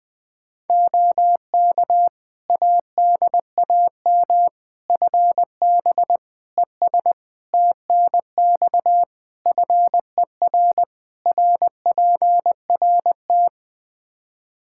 Morse code, heard as a keyed tone, about 20 wpm, 700 Hz. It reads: OK ADAM FB ES TNX FER RPRT
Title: Practice QSO - Exchange 2 - Info - 20wpm